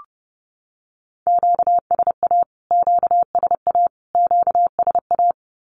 Morse code, heard as a keyed tone, about 30 wpm, 700 Hz. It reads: QSA QSA QSA
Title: Q-Codes Repeated 3x 30wpm